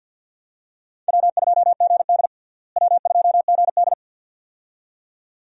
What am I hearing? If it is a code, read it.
Morse code, heard as a keyed tone, about 52 wpm, 700 Hz. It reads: W2GD W2GD
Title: USA Call Signs - 4 Characters - Repeated 2x - 50wpm